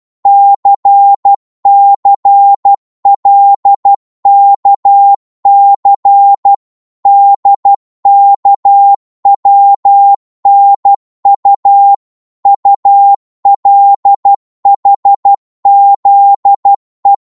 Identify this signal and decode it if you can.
Morse code, heard as a keyed tone, about 12 wpm, 800 Hz. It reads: CCLKC DKWNU ULHZE
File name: Morse / ltr-12wpm